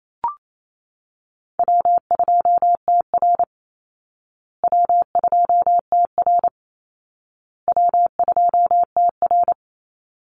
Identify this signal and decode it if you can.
Morse code, heard as a keyed tone, about 28 wpm, 700 Hz. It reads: W2TR W2TR W2TR
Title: Easy Call Signs Repeated 3x - 28wpm